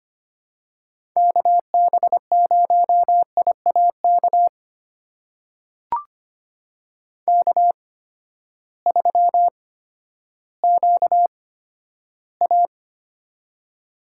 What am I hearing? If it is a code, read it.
Morse code, heard as a keyed tone, about 25 wpm, spaced as for 12 wpm, 700 Hz. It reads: KB0IAK K3QA
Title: USA Call Signs - 8x Character Spacing - 25wpm